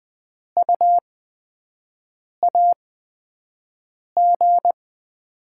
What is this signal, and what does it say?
Morse code, heard as a keyed tone, about 20 wpm, 700 Hz. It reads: U A G
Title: Top 20 QSO Elements - 8x Character Spacing - 20wpm